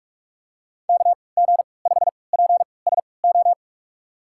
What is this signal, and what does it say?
Morse code, heard as a keyed tone, about 44 wpm, spaced as for 25 wpm, 700 Hz. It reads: KG5PSO
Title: USA Call Signs - 3x Character Spacing - 45wpm